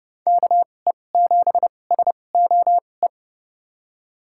Morse code, heard as a keyed tone, about 32 wpm, spaced as for 22 wpm, 700 Hz. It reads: KE7SOE
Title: USA Call Signs - 2x Character Spacing - 30wpm